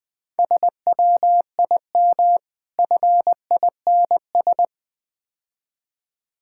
Morse code, heard as a keyed tone, about 21 wpm, 700 Hz. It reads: SWIM FINS
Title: Difficult Words - Speed-Racing with Hints - 20wpm